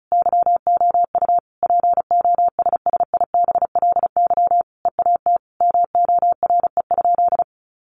Morse code, heard as a keyed tone, about 34 wpm, 700 Hz. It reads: YOU POSSIBLY EAT MORE?
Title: Sentences from Top 1300 Words 35wpm